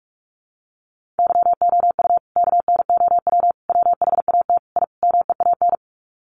Morse code, heard as a keyed tone, about 44 wpm, 700 Hz. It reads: YOU KNOW WHAT I MEAN
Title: Sets of 5 Words 45wpm